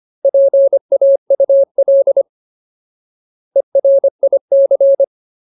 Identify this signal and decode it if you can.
Morse code, heard as a keyed tone, about 25 wpm, 550 Hz. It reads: PAUL ERIC
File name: Ultimate Word List / 25WPM Ultimate